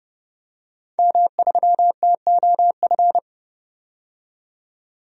Morse code, heard as a keyed tone, about 30 wpm, 700 Hz. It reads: M3TOF
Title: DXCC-100 Call Signs - 5 Characters - 30wpm